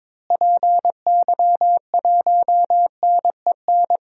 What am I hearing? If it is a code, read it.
Morse code, heard as a keyed tone, about 22 wpm, 700 Hz. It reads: PY1NEN
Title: DXCC-100 Call Signs - 6 Characters - Repeated 3x - 22wpm